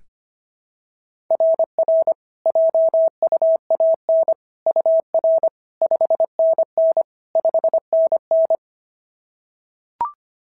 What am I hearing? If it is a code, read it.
Morse code, heard as a keyed tone, about 22 wpm, 650 Hz. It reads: RR JUAN UR 5NN 5NN
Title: SOTA QSO Practice 25wpm